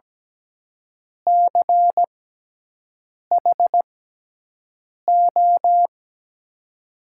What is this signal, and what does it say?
Morse code, heard as a keyed tone, about 17 wpm, spaced as for 6 wpm, 700 Hz. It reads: CHO
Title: USA Call Signs - 6 Characters - 6x Character Spacing - 17wpm